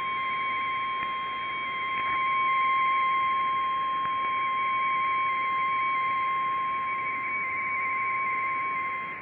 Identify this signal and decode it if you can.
Two-tone teleprinter signal, about 170 Hz shift.
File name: BiTx / mp3bitx / 1140 UA9NHK rtty2